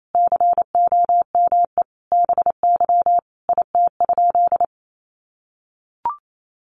Morse code, heard as a keyed tone, about 28 wpm, 700 Hz. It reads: COME BY IT?
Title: Sentences from Top 100 Words 28wpm